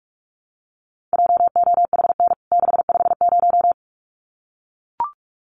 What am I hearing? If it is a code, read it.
Morse code, heard as a keyed tone, about 44 wpm, 700 Hz. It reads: JOHN 650
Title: CWT Exchange 45wpm